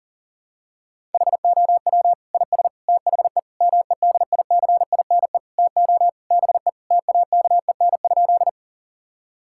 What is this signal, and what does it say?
Morse code, heard as a keyed tone, about 40 wpm, 700 Hz. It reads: HOW IS THE MEDICINE TO BE TAKEN?
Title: Sentences from Top 1000 Words - Review Part 6 - 40wpm